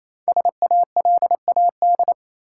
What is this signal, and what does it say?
Morse code, heard as a keyed tone, about 28 wpm, 700 Hz. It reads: SALAD
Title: Five Letter Words - 28wpm